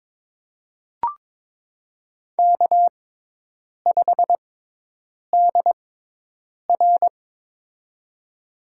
Morse code, heard as a keyed tone, about 22 wpm, spaced as for 8 wpm, 700 Hz. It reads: K5DR
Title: USA Call Signs - 4 Characters - 6x Character Spacing - 22wpm